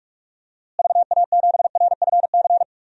Morse code, heard as a keyed tone, about 45 wpm, 700 Hz. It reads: VA7RFC
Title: DXCC-100 Call Signs - 6 Characters - Repeated 3x - 45wpm